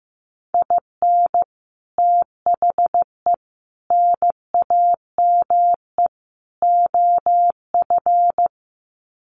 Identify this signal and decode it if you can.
Morse code, heard as a keyed tone, about 15 wpm, 700 Hz. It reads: IN THE NAME OF
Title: Sets of 4 Words 15wpm